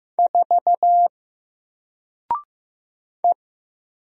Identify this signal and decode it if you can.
Morse code, heard as a keyed tone, about 15 wpm, 700 Hz. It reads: 4 E
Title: Single Letter-Number 15wpm